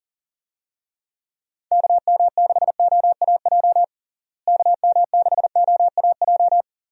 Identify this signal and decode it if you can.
Morse code, heard as a keyed tone, about 40 wpm, 700 Hz. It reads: KM6OAJ KM6OAJ
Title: USA Call Signs - 6 Characters - Repeated 3x - 40wpm